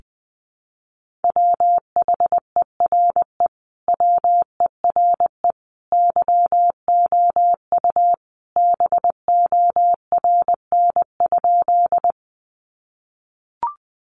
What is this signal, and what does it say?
Morse code, heard as a keyed tone, about 19 wpm, 700 Hz. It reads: WHERE WERE YOU BORN?
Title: Sentences from Top 1000 Words 20wpm